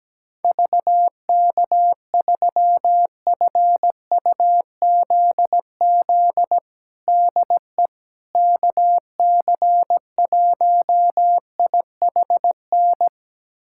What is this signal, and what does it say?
Morse code, heard as a keyed tone, about 17 wpm, 700 Hz. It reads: VK3FUZZ DE KC1IHN
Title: USA Callsign DE Callsign 17wpm